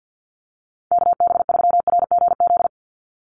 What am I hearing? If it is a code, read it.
Morse code, heard as a keyed tone, about 50 wpm, 700 Hz. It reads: KB3RGZ
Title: USA Call Signs 50wpm